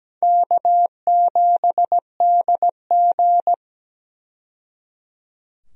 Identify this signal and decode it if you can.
Morse code, heard as a keyed tone, about 17 wpm, 700 Hz. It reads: K7DG
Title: USA Call Signs - 4 Characters - 17wpm